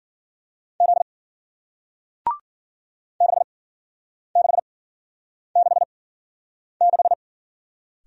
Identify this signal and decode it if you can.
Morse code, heard as a keyed tone, about 50 wpm, spaced as for 9 wpm, 700 Hz. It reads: Z 6666
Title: Speed-Racing Single Letter-Number 40wpm